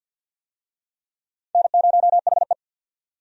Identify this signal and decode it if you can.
Morse code, heard as a keyed tone, about 47 wpm, 700 Hz. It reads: N0HE
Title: USA Call Signs - 4 Characters - 50wpm